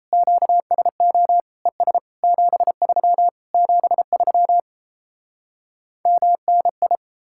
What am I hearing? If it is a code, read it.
Morse code, heard as a keyed tone, about 38 wpm, 700 Hz. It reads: QSO ES 73 73 MNI